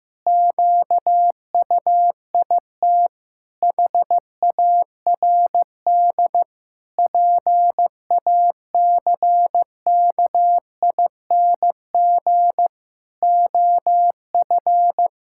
Morse code, heard as a keyed tone, about 16 wpm, 700 Hz. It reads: QUIT HARD PACKING OF
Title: Pangram Challenge v2 - 15wpm